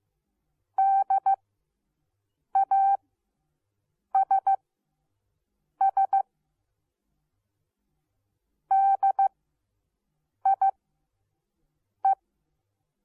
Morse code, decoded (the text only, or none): DASS DIE